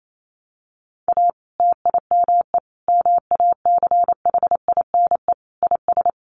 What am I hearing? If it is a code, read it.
Morse code, heard as a keyed tone, about 28 wpm, 700 Hz. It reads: A TIME MACHINE IS
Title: A Sound Of Thunder 28wpm